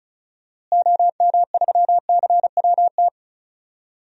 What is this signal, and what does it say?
Morse code, heard as a keyed tone, about 36 wpm, 700 Hz. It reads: OM3CWT
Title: DXCC-100 Call Signs - 6 Characters - Repeated 2x - 35wpm